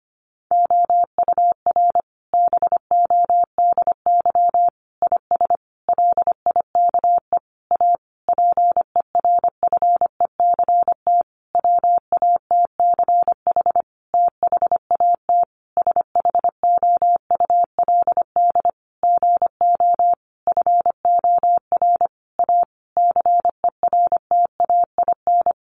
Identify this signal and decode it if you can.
Morse code, heard as a keyed tone, about 25 wpm, 700 Hz. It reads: OUR BODY IS LIKE A PERFECT WATCH THAT SHOULD GO FOR A CERTAIN
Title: Sentences from Top 1000 Words 25wpm